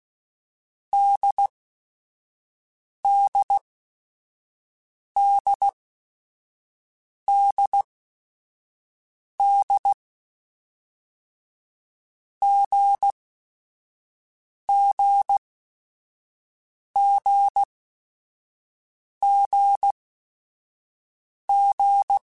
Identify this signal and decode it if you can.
Morse code, heard as a keyed tone, about 16 wpm, spaced as for 5 wpm, 800 Hz. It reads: DDDDD GGGGG